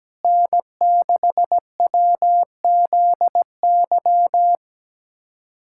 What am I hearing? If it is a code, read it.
Morse code, heard as a keyed tone, about 17 wpm, 700 Hz. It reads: N6WZY
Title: USA Callsign DE Callsign - 5 Characters - 17wpm